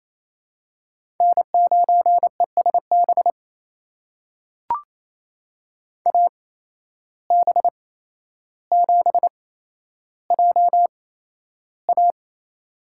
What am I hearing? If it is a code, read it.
Morse code, heard as a keyed tone, about 27 wpm, spaced as for 12 wpm, 700 Hz. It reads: N9ESB AB7JA
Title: USA Call Signs - 5 Characters - 8x Character Spacing - 28wpm